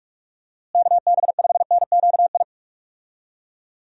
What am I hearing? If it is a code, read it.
Morse code, heard as a keyed tone, about 42 wpm, 700 Hz. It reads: KB5NQI
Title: USA Call Signs - 6 Characters - Repeated 2x - 45wpm